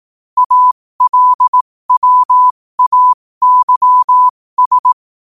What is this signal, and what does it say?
Morse code, heard as a keyed tone, about 18 wpm, spaced as for 15 wpm, 1000 Hz. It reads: ALWAYS
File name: qotdfiles / 15wpm